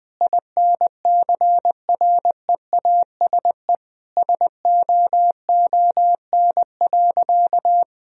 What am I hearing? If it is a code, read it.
Morse code, heard as a keyed tone, about 21 wpm, 700 Hz. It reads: INCREASE SOON.
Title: Sentences from Top 1000 Words - Review Part 4 - 20wpm